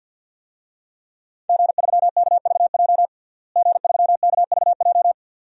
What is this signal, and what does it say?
Morse code, heard as a keyed tone, about 52 wpm, 700 Hz. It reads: G3KVJ G3KVJ
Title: DXCC-100 Call Signs - 5 Characters - Repeated 2x - 50wpm